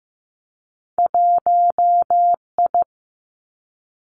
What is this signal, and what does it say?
Morse code, heard as a keyed tone, about 15 wpm, 700 Hz. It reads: 1 I
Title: Two Challenging Letter-Number Combinations 15wpm